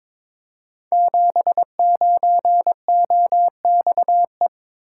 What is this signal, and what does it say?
Morse code, heard as a keyed tone, about 23 wpm, 700 Hz. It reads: 79OXE
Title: Code Groups - 5 Letter:Numbers - 22wpm